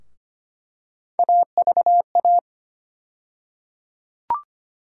Morse code, heard as a keyed tone, about 25 wpm, 700 Hz. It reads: AVA
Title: USA Call Sign Postfixes - 3 Letters 25wpm